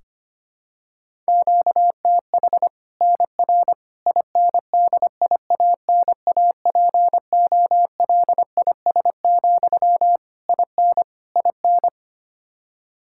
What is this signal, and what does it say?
Morse code, heard as a keyed tone, about 25 wpm, 700 Hz. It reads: QTH NR INDIANAPOLIS, IN IN
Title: Practice QSO - Exchange 1 - Beginning - 25wpm